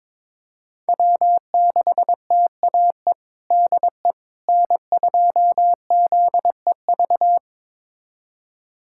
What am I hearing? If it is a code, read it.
Morse code, heard as a keyed tone, about 22 wpm, 700 Hz. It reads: W6TAE DE N2ZEV
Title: USA Callsign DE Callsign - 5 Characters - 22wpm